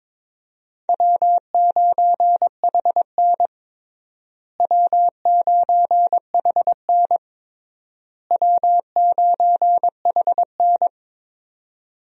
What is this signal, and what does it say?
Morse code, heard as a keyed tone, about 22 wpm, 700 Hz. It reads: W9HN W9HN W9HN
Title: USA Call Signs - 4 Characters - Repeated 3x - 22wpm